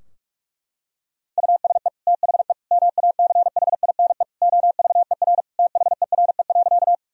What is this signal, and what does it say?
Morse code, heard as a keyed tone, about 46 wpm, 700 Hz. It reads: USE THE MACHINE OVER THERE.
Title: Sentences from Top 1000 Words - Review Part 2 - 45wpm